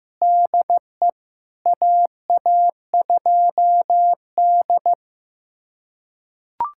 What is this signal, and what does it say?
Morse code, heard as a keyed tone, about 15 wpm, 700 Hz. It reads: DE AA2D
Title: USA Callsign DE Callsign - 4 Characters - 15wpm